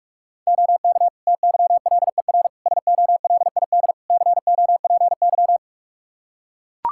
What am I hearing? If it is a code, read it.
Morse code, heard as a keyed tone, about 45 wpm, 700 Hz. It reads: OK TYLER SOLID COPY
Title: Practice QSO - Exchange 3 - More Info - 45wpm